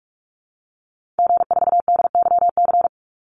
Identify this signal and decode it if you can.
Morse code, heard as a keyed tone, about 44 wpm, 700 Hz. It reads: G4DYC
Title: Call Signs Repeated 2x - 45wpm